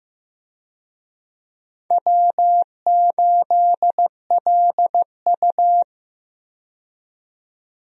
Morse code, heard as a keyed tone, about 16 wpm, 700 Hz. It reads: W8LU
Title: USA Call Signs - 4 Characters - 15wpm